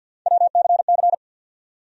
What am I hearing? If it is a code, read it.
Morse code, heard as a keyed tone, about 52 wpm, 700 Hz. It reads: WCC
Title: USA Call Sign Postfixes - 3 Letters 50wpm